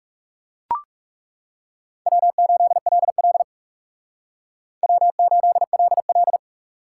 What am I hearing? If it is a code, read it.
Morse code, heard as a keyed tone, about 37 wpm, 700 Hz. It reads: W8LL W8LL
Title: Speed Racing - USA 1x2 Call Signs 30wpm